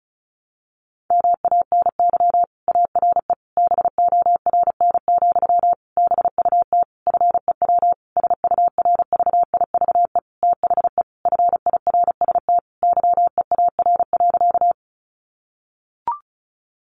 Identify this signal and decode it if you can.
Morse code, heard as a keyed tone, about 34 wpm, 700 Hz. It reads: MANY ARE BORN, BUT FEW SURVIVE THE FIRST YEAR.
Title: Sentences from Top 1300 Words 35wpm